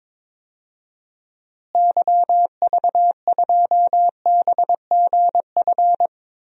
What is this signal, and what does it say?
Morse code, heard as a keyed tone, about 22 wpm, 700 Hz. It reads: YV2BGF
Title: DXCC-100 Call Signs - 6 Characters - Repeated 2x - 22wpm